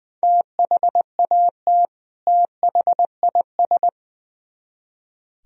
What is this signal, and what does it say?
Morse code, heard as a keyed tone, about 21 wpm, 700 Hz. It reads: THAT THIS
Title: Speed-Racing - Sets of 2 Words 20wpm